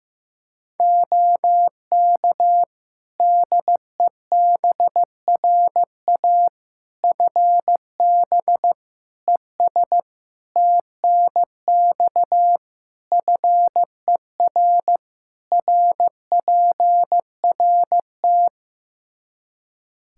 Morse code, heard as a keyed tone, about 15 wpm, 700 Hz. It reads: OK DEBRA FB ES TNX FER RPRT
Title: Practice QSO - Exchange 2 - Info - 15wpm